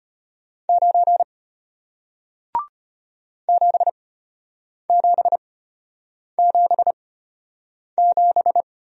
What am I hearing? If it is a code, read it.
Morse code, heard as a keyed tone, about 33 wpm, spaced as for 8 wpm, 700 Hz. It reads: 9 7777